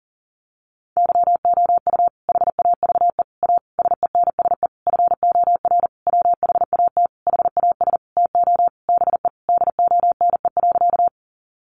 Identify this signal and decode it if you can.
Morse code, heard as a keyed tone, about 40 wpm, 700 Hz. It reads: YOU HAVE A SENSE FOR WHAT HAS TO BE DONE.
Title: Sentences from Top 700 Words 40wpm